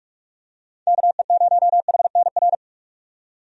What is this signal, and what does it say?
Morse code, heard as a keyed tone, about 46 wpm, 700 Hz. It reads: KE0HNR
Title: USA Call Signs - 6 Characters - 45wpm